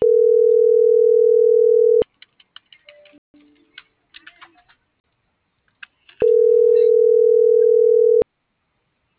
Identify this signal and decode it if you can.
Ringback tone (North America), 440 + 480 Hz.